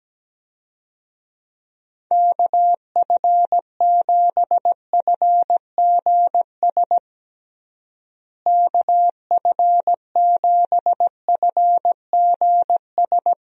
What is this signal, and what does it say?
Morse code, heard as a keyed tone, about 17 wpm, 700 Hz. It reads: KF7FGS KF7FGS
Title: USA Call Signs - 6 Characters - Repeated 3x - 17wpm